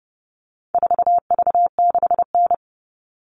Morse code, heard as a keyed tone, about 30 wpm, 700 Hz. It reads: 4V6N
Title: All Cut Numbers - 4-Digit 30wpm